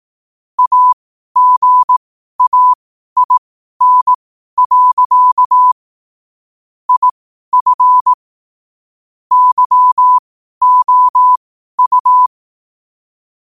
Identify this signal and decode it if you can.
Morse code, heard as a keyed tone, about 18 wpm, spaced as for 12 wpm, 1000 Hz. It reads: AGAIN. IF YOU
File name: qotdfiles / 12wpm